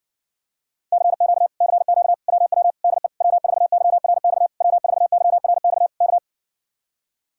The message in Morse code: CQ CQ WW DE W4YRX W4YRX K

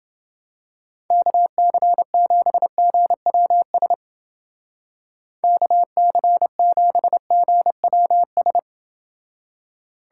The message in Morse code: KC7GWS KC7GWS

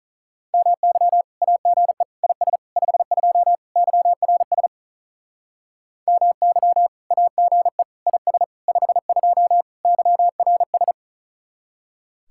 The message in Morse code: MY AGE IS 52 YRS MY AGE IS 52 YRS

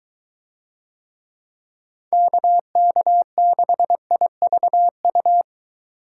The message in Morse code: KK6IVU

